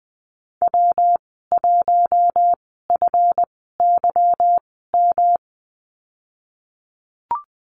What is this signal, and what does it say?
Morse code, heard as a keyed tone, about 20 wpm, spaced as for 15 wpm, 700 Hz. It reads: W1FYM
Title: 179 Lesson 29g - Slash - Call Signs 20wpm